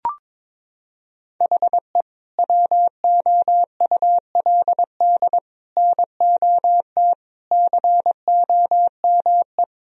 Morse code, heard as a keyed tone, about 22 wpm, 700 Hz. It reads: HE WOULD NOT COME
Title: Sentences from Top 1000 Words - Review Part 1 - 22wpm